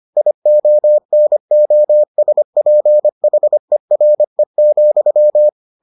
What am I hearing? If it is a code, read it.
Morse code, heard as a keyed tone, about 25 wpm, 600 Hz. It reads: IONOSPHERE,